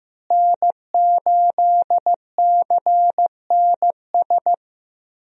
Morse code, heard as a keyed tone, about 16 wpm, 700 Hz. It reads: N8CNS
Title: USA Call Signs - 5 Characters - Repeated 3x - 15wpm